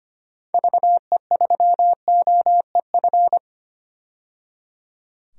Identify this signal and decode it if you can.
Morse code, heard as a keyed tone, about 25 wpm, 690 Hz. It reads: VE3OEF
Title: DXCC-100 Call Signs - 6 Characters - 25wpm